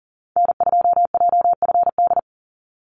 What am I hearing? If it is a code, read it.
Morse code, heard as a keyed tone, about 40 wpm, 700 Hz. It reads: N2JFD
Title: Call Signs Repeated 3x - 40wpm